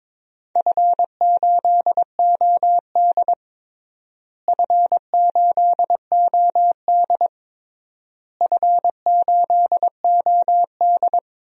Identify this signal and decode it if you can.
Morse code, heard as a keyed tone, about 22 wpm, 700 Hz. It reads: F8OD F8OD F8OD
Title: DXCC-100 Call Signs - 4 Characters - Repeated 3x - 22wpm